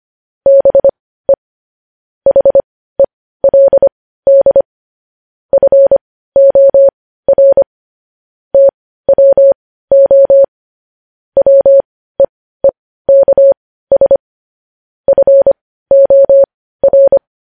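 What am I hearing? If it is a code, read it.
Morse code, heard as a keyed tone, about 25 wpm, spaced as for 15 wpm, 550 Hz. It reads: BE HELD FOR TWO WEEKS FOR
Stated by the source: Morse code news headlines sent at 15, opening with vvv.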